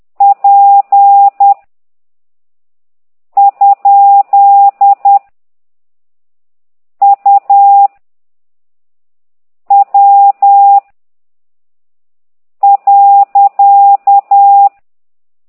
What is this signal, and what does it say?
Morse code, heard as a keyed tone, about 10 wpm, 800 Hz. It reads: P ? U W .
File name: ct2idw / exerc9